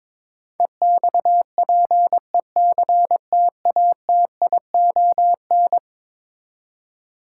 Morse code, heard as a keyed tone, about 21 wpm, 700 Hz. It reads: EXPECTATION
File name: practice / 22wpm / Eleven Letter Words - 22wpm